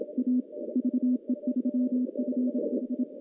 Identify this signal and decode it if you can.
Morse code, heard as a keyed tone, about 27 wpm, 250 Hz. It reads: A VE3FH
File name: contests / CW-SS-k5tr- -0200